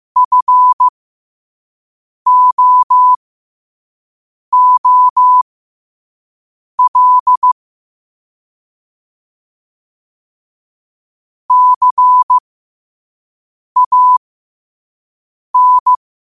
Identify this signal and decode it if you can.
Morse code, heard as a keyed tone, about 15 wpm, spaced as for 5 wpm, 1000 Hz. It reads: FOOL CAN
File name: qotdfiles / 5wpm